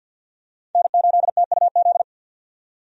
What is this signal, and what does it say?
Morse code, heard as a keyed tone, about 52 wpm, 700 Hz. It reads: N8TUZ
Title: USA Call Signs - 5 Characters - 50wpm